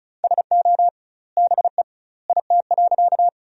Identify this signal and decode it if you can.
Morse code, heard as a keyed tone, about 34 wpm, 700 Hz. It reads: SO BE IT.
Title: 2x Word-Spacing - Sentences from Top 100 Words - 35wpm